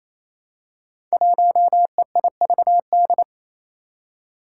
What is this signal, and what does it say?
Morse code, heard as a keyed tone, about 28 wpm, 700 Hz. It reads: 1EIVD
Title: Five Challenging Letter-Number Combinations 28wpm